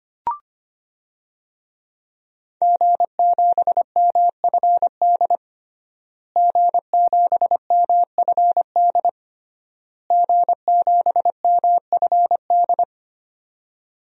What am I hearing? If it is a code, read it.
Morse code, heard as a keyed tone, about 25 wpm, 700 Hz. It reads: G7MFD G7MFD G7MFD
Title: DXCC-100 Call Signs - 5 Characters - Repeated 3x - 25wpm